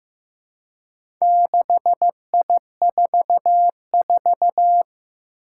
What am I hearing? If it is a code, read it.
Morse code, heard as a keyed tone, about 15 wpm, 700 Hz. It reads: 6I44
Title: Four Challenging Letter-Number Combinations 15wpm